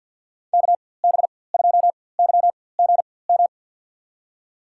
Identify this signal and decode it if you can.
Morse code, heard as a keyed tone, about 51 wpm, spaced as for 23 wpm, 700 Hz. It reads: KB2YGM
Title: USA Call Signs - 6 Characters - 4x Character Spacing - 50wpm